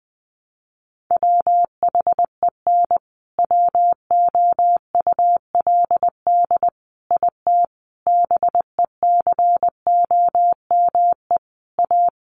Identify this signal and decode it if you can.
Morse code, heard as a keyed tone, about 20 wpm, 700 Hz. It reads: WHEN WOULD IT BECOME A
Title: Sentences from Top 1200 Words 20wpm